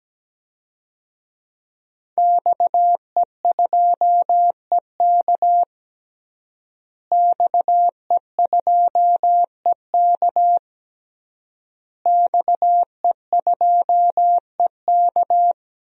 Morse code, heard as a keyed tone, about 17 wpm, 700 Hz. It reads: XE2EK XE2EK XE2EK
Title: DXCC-100 Call Signs - 5 Characters - Repeated 3x - 17wpm